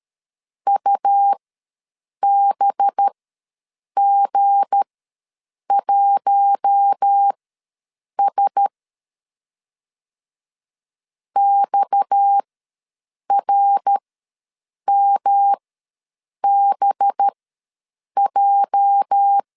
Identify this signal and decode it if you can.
Morse code, heard as a keyed tone, about 13 wpm, spaced as for 7 wpm, 800 Hz. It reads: UBG1S XRMBJ
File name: LEZIONI / 8 YJ45 / 8 YJ ws 12wpm 3342